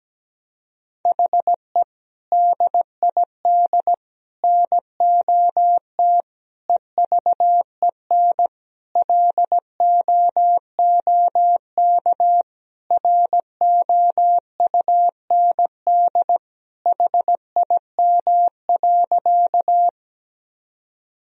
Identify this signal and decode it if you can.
Morse code, heard as a keyed tone, about 17 wpm, 700 Hz. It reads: HE DID NOT EVEN LOOK ROUND HIM.
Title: Sentences from Top 1000 Words - Review Part 1 - 17wpm